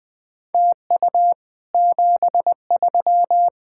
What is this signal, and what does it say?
Morse code, heard as a keyed tone, about 22 wpm, 700 Hz. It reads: TU 73